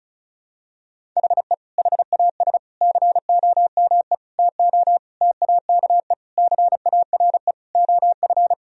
Morse code, heard as a keyed tone, about 34 wpm, 700 Hz. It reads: HE HAS COME TO TAKE CARE OF
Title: Sentences from Top 1000 Words - Review Part 1 - 35wpm